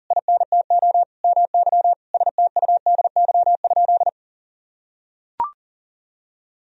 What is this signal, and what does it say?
Morse code, heard as a keyed tone, about 40 wpm, 700 Hz. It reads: INTO MY STUDY?
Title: Sentences from Top 1000 Words - Review Part 1 - 40wpm